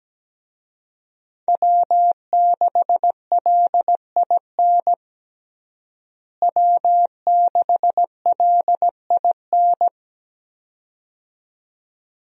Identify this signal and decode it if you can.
Morse code, heard as a keyed tone, about 17 wpm, 700 Hz. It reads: W6LIN W6LIN
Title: USA Call Signs - 5 Characters - Repeated 2x - 17wpm